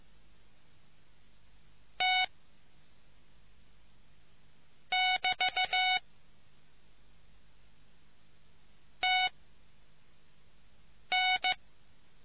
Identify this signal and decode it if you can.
Morse code, heard as a keyed tone, about 15 wpm, 2200 Hz. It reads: T <BT> T N